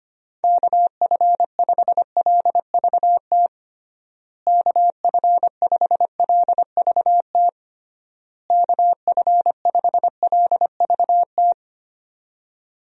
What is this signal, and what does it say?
Morse code, heard as a keyed tone, about 25 wpm, 700 Hz. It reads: KF5LVT KF5LVT KF5LVT
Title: USA Call Signs - 6 Characters - Repeated 3x - 25wpm